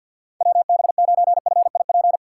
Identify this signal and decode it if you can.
Morse code, heard as a keyed tone, about 52 wpm, 700 Hz. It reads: WB9FIP
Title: K1USN SST QSO Practice - 50wpm